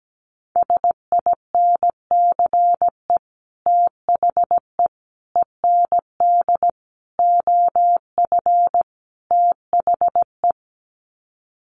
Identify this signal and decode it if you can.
Morse code, heard as a keyed tone, about 17 wpm, 700 Hz. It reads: SINCE THE END OF THE
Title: Sets of 5 Words Encore 17wpm